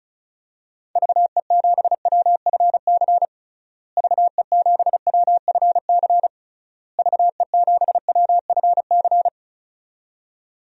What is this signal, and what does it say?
Morse code, heard as a keyed tone, about 36 wpm, 700 Hz. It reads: VE7WFC VE7WFC VE7WFC
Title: DXCC-100 Call Signs - 6 Characters - Repeated 3x - 35wpm